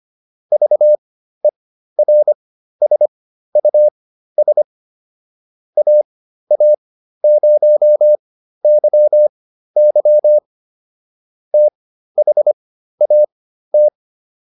Morse code, heard as a keyed tone, about 25 wpm, spaced as for 13 wpm, 600 Hz. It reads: VERSUS AA0YY THAT